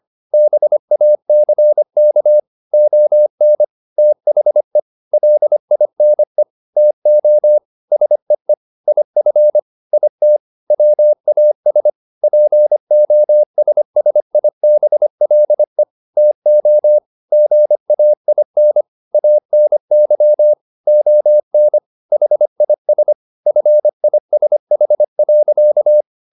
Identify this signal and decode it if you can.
Morse code, heard as a keyed tone, about 25 wpm, 600 Hz. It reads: BACK ON THE LINE TO SEE IF IT WAS POSSIBLE TO GAIN ANY ON HIS FISH.